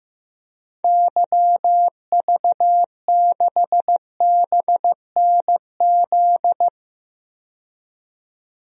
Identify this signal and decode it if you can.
Morse code, heard as a keyed tone, about 15 wpm, 700 Hz. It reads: YV6BNZ
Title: DXCC-100 Call Signs - 6 Characters - 15wpm